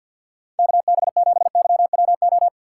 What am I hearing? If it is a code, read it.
Morse code, heard as a keyed tone, about 51 wpm, 700 Hz. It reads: KB7YWO
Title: USA Call Signs - 4x Character Spacing - 50wpm